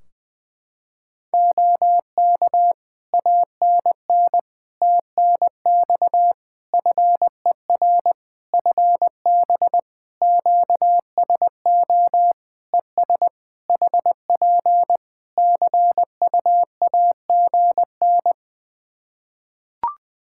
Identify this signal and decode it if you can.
Morse code, heard as a keyed tone, about 20 wpm, 700 Hz. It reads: OK ANN TNX FER FB QSO ES HP CUAGN
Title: Practice QSO - Exchange 4 - Ending - 20wpm